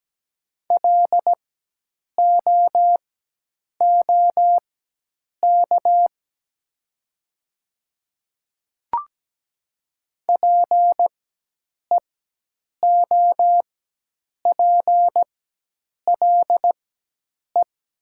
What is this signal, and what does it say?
Morse code, heard as a keyed tone, about 17 wpm, spaced as for 8 wpm, 700 Hz. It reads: LOOK PEOPLE
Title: 4x Character Spacing - Top 100 Words - 17wpm